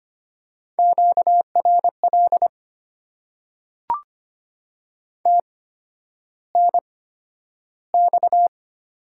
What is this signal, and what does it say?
Morse code, heard as a keyed tone, about 25 wpm, spaced as for 11 wpm, 700 Hz. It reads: QRL TNX
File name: Top 90 QSO Elements - 8x Character Spacing - 25wpm